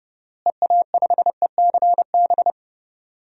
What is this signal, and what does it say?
Morse code, heard as a keyed tone, about 32 wpm, 700 Hz. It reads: EA5ECB
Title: DXCC-100 Call Signs - 6 Characters - Repeated 3x - 30wpm